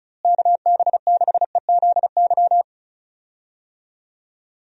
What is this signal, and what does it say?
Morse code, heard as a keyed tone, about 36 wpm, 700 Hz. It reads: KB6EZY
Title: USA Call Signs - 6 Characters - Repeated 3x - 35wpm